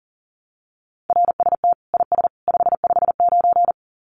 Morse code, heard as a keyed tone, about 40 wpm, 700 Hz. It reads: RST IS 559
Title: Common QSO Phrases 40wpm